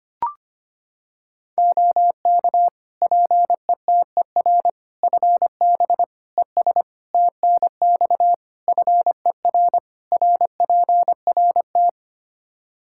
Morse code, heard as a keyed tone, about 25 wpm, 700 Hz. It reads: OK PETER FB ES TNX FER RPRT
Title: K1USN SST QSO Practice 25wpm